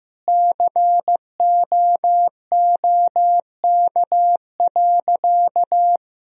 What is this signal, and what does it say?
Morse code, heard as a keyed tone, about 16 wpm, 700 Hz. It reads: COOK.
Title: Sentences from Top 1000 Words - Review Part 4 - 15wpm